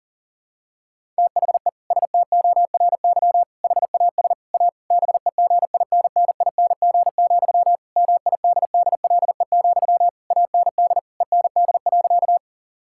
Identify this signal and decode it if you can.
Morse code, heard as a keyed tone, about 40 wpm, 700 Hz. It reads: THE STORY HAS A BEGINNING, MIDDLE, AND END.
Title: Sentences from Top 1000 Words - Review Part 6 - 40wpm